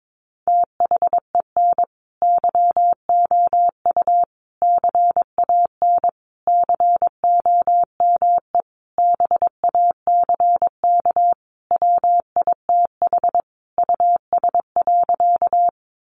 Morse code, heard as a keyed tone, about 22 wpm, 700 Hz. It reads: THEN YOU CAN COME BACK WITH US.
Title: A Sound Of Thunder 22wpm